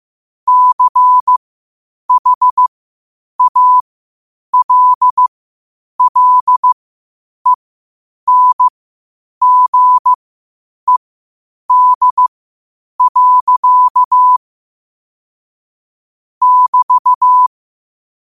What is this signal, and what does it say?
Morse code, heard as a keyed tone, about 15 wpm, spaced as for 8 wpm, 1000 Hz. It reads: CHALLENGED. <BT>
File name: qotdfiles / 8wpm